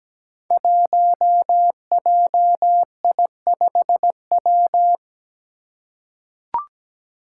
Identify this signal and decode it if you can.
Morse code, heard as a keyed tone, about 17 wpm, 700 Hz. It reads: 1JI5W
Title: Five Challenging Letter-Number Combinations 17wpm